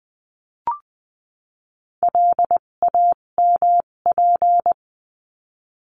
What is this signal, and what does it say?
Morse code, heard as a keyed tone, about 20 wpm, spaced as for 17 wpm, 700 Hz. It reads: LAMP
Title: 257 Lesson 42b - Words 14-20wpm